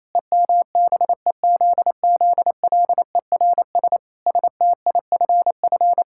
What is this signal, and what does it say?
Morse code, heard as a keyed tone, about 28 wpm, 700 Hz. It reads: EMBEZZLERS STIFF
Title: Pangram Challenge v2 - 28wpm